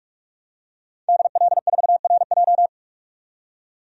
Morse code, heard as a keyed tone, about 46 wpm, 700 Hz. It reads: DL4RJ
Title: DXCC-100 Call Signs - 5 Characters - 45wpm